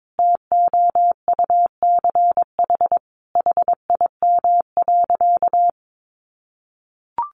Morse code, heard as a keyed tone, about 22 wpm, 700 Hz. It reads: TOUCH HIM.
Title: Sentences from Top 1300 Words 22wpm